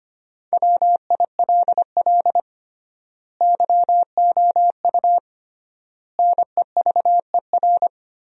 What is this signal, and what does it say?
Morse code, heard as a keyed tone, about 25 wpm, 700 Hz. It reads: WILL YOU NEVER
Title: 3x Word-Spacing - Sentences from Top 300 Words - 25wpm